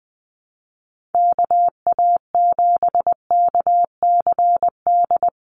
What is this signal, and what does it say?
Morse code, heard as a keyed tone, about 20 wpm, 700 Hz. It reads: KA7KCD
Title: USA Call Signs Repeated 3x 20wpm